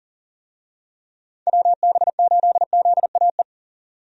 Morse code, heard as a keyed tone, about 37 wpm, 700 Hz. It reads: WB8ZAE
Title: USA Call Signs - 6 Characters - 40wpm